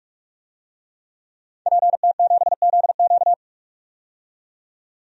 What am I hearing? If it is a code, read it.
Morse code, heard as a keyed tone, about 45 wpm, 700 Hz. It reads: PT7ZQ
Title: DXCC-100 Call Signs - 5 Characters - 45wpm